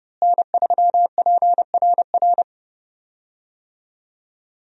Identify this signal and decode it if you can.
Morse code, heard as a keyed tone, about 30 wpm, 700 Hz. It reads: N3PRR
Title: USA Call Signs - 5 Characters - Repeated 3x - 30wpm